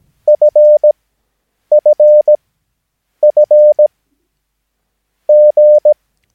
Morse code, heard as a keyed tone, about 17 wpm, spaced as for 7 wpm, 600 Hz. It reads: FFFG
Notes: Övningsmaterial med alfabetet A till Ö samt siffror 0 till 9 enligt International/Continental Morse Code.